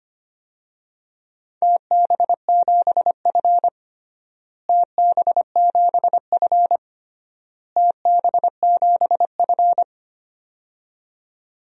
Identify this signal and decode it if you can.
Morse code, heard as a keyed tone, about 25 wpm, 700 Hz. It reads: TB7F TB7F TB7F
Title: DXCC-100 Call Signs - 4 Characters - Repeated 3x - 25wpm